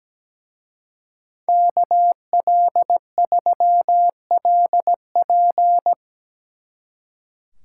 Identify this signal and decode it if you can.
Morse code, heard as a keyed tone, about 18 wpm, 700 Hz. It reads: KL3LP
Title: Speed Racing - USA 2x2 Call Signs 17wpm